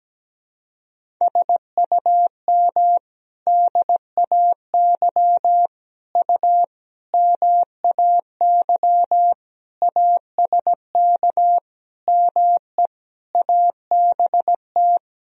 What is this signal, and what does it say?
Morse code, heard as a keyed tone, about 17 wpm, 700 Hz. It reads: SUM DAY U MAY ASK ME ABT
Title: Sentences from Top 200 Words and Abbreviations 17wpm